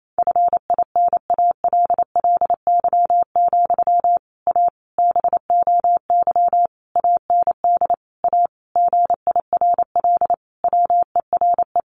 Morse code, heard as a keyed tone, about 28 wpm, 700 Hz. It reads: FINALLY, A BOY AND A GIRL WERE
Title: Sentences from Top 1200 Words 28wpm